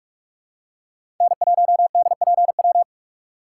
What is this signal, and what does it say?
Morse code, heard as a keyed tone, about 44 wpm, 700 Hz. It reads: N1DPW
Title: USA Call Signs - 5 Characters - 2x Character Spacing - 45wpm